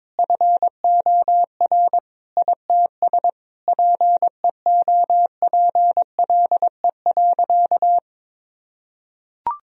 Morse code, heard as a keyed tone, about 22 wpm, 700 Hz. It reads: FOR ITS PEOPLE.
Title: Sentences from Top 1000 Words - Review Part 3 - 22wpm